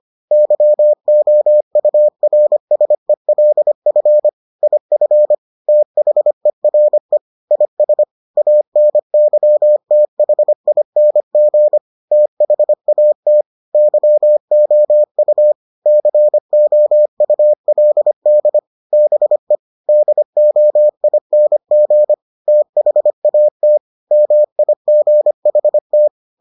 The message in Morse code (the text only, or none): YOURSELF IF THERE IS ANYTHING THAT YOU COULD BE DOING THAT MIGHT